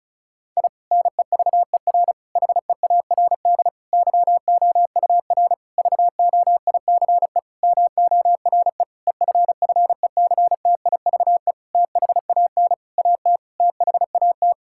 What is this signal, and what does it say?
Morse code, heard as a keyed tone, about 34 wpm, 700 Hz. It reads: I NEVER HEARD YOUR VOICE MORE EFFECTIVE THAN AT THAT
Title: Sentences from Top 1000 Words - Review Part 6 - 35wpm